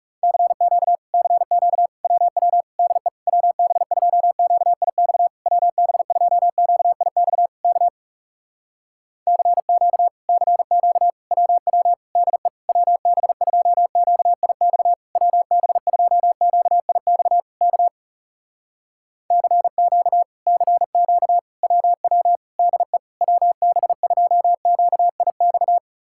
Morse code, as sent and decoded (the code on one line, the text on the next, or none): -.-. --.- / -.-. --.- / .-- .-- / -.. / .-- -... ..--- --.- .. -..- / .-- -... ..--- --.- .. -..- / -.- / -.-. --.- / -.-. --.- / .-- .-- / -.. / .-- -... ..--- --.- .. -..- / .-- -... ..--- --.- .. -..- / -.- / -.-. --.- / -.-. --.- / .-- .-- / -.. / .-- -... ..--- --.- .. -..-
CQ CQ WW DE WB2QIX WB2QIX K CQ CQ WW DE WB2QIX WB2QIX K CQ CQ WW DE WB2QIX